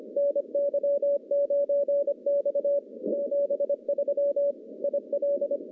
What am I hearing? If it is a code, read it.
Morse code, heard as a keyed tone, about 25 wpm, 570 Hz. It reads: NY9X 73 IL
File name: contests / CW-SS-k5tr- -0100